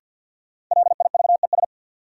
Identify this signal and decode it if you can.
Morse code, heard as a keyed tone, about 51 wpm, 700 Hz. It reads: LIVES
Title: Sets of 5 Words Encore - New Word Drill - 50wpm